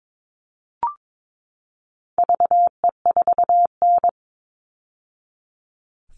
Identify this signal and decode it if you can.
Morse code, heard as a keyed tone, about 22 wpm, 700 Hz. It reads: VE4N
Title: All Cut Numbers - 4-Digit 22wpm